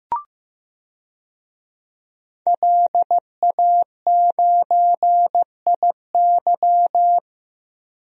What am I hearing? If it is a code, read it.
Morse code, heard as a keyed tone, about 15 wpm, 700 Hz. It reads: LA9IY
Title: DXCC-100 Call Signs - 5 Characters - Repeated 2x - 15wpm